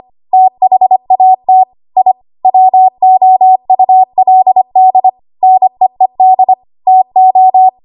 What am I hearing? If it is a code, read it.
Morse code, heard as a keyed tone, about 25 wpm, 750 Hz. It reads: THAT I WOULD NEED TO
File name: Archive / 25 WPM / 25WPM